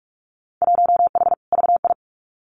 Morse code, heard as a keyed tone, about 46 wpm, 700 Hz. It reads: 1H VI
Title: Winter Field Day Exchanges Encore 45wpm